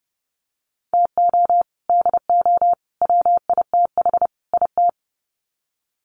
Morse code, heard as a keyed tone, about 30 wpm, 700 Hz. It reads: TO DO WITH IT
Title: Sets of 4 Words 30wpm